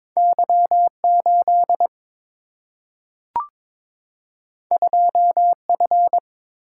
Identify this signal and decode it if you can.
Morse code, heard as a keyed tone, about 22 wpm, 700 Hz. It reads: Y 8 2 F
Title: Single-Letter Single-Number 22wpm